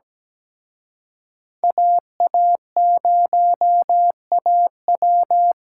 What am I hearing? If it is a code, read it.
Morse code, heard as a keyed tone, about 20 wpm, 700 Hz. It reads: AA0AW
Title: Speed Racing - USA 2x2 Call Signs 17wpm